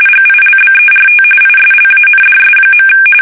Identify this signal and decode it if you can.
RTTY, about 45.45 baud. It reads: GILBERT ISLANDS.
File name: fsk850 / USS-Massachusetts